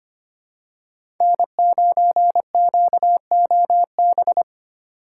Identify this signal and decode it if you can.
Morse code, heard as a keyed tone, about 25 wpm, 700 Hz. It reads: N9QOB
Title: USA Call Signs - 5 Characters - 3x Character Spacing - 25wpm